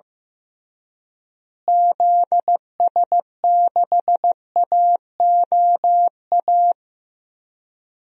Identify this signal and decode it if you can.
Morse code, heard as a keyed tone, about 15 wpm, 700 Hz. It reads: ZS6AOA
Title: DXCC-100 Call Signs - 6 Characters - Repeated 3x - 15wpm